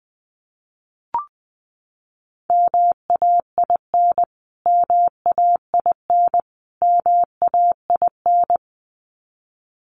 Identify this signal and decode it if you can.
Morse code, heard as a keyed tone, about 20 wpm, 700 Hz. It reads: MAIN MAIN MAIN
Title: Top 500 Words Repeated 20wpm